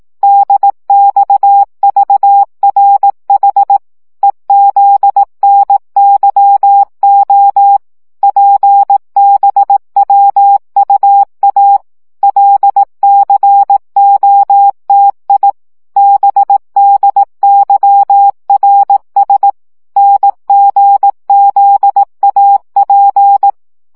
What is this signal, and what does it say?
Morse code, heard as a keyed tone, about 18 wpm, 800 Hz. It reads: DXVRH EZNYO PBWUA LCOTI BDYRS NGZAP
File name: morze vizsga 2011-3 15